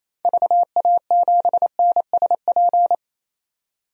Morse code, heard as a keyed tone, about 28 wpm, 700 Hz. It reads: VA7NSP
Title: DXCC-100 Call Signs - 6 Characters - 28wpm